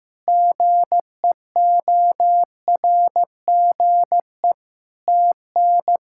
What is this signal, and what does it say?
Morse code, heard as a keyed tone, about 15 wpm, 700 Hz. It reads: GEORGE TN
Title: K1USN SST QSO Practice - 15wpm